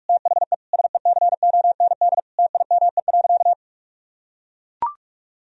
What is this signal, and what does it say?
Morse code, heard as a keyed tone, about 41 wpm, 700 Hz. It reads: THE SECOND TIME.
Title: Sentences from Top 1000 Words - Review Part 1 - 45wpm